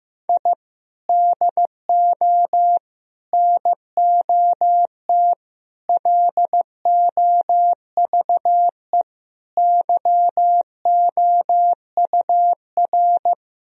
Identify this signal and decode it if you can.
Morse code, heard as a keyed tone, about 15 wpm, 700 Hz. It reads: I DO NOT LOVE YOUR
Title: Sentences from Top 1000 Words - Review Part 4 - 15wpm